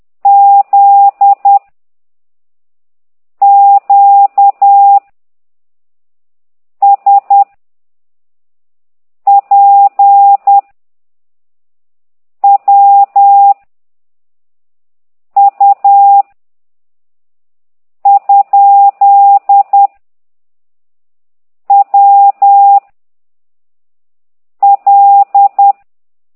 Morse code, heard as a keyed tone, about 10 wpm, 800 Hz. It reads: Z Q S P W U ? W L